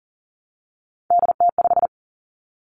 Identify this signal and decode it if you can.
Morse code, heard as a keyed tone, about 41 wpm, 700 Hz. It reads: DT5
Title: Three Challenging Letter-Number Combinations 40wpm